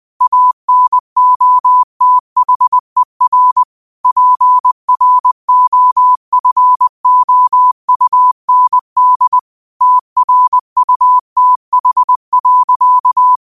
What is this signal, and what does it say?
Morse code, heard as a keyed tone, about 20 wpm, 1000 Hz. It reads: ANOTHER PROFOUND TRUTH.
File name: qotdfiles / 20wpm